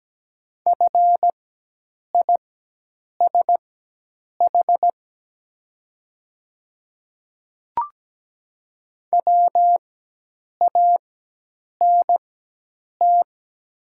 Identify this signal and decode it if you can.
Morse code, heard as a keyed tone, about 17 wpm, spaced as for 8 wpm, 700 Hz. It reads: FISH WANT
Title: 4x Character Spacing - Top 300 Words - 17wpm